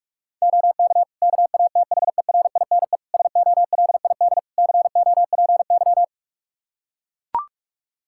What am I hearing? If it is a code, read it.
Morse code, heard as a keyed tone, about 41 wpm, 700 Hz. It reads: OK KATHERINE SOLID COPY
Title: Practice QSO - Exchange 3 - More Info - 45wpm